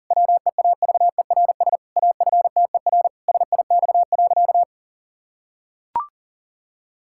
Morse code, heard as a keyed tone, about 40 wpm, 700 Hz. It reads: WEAVERS AFTER SIX.
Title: Pangram Challenge v2 - 40wpm